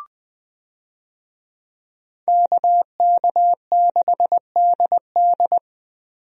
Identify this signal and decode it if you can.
Morse code, heard as a keyed tone, about 20 wpm, 700 Hz. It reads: KK6DD
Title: USA Call Signs - 5 Characters - Repeated 2x - 20wpm